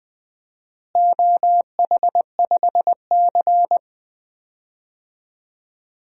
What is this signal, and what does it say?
Morse code, heard as a keyed tone, about 20 wpm, 700 Hz. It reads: OH5C
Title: DXCC-100 Call Signs - 4 Characters - Repeated 3x - 20wpm